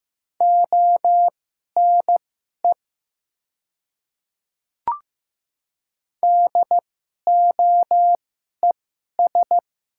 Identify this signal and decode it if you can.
Morse code, heard as a keyed tone, about 15 wpm, spaced as for 11 wpm, 700 Hz. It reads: ONE DOES
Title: 2x Character Spacing - Top 500 Words - 15wpm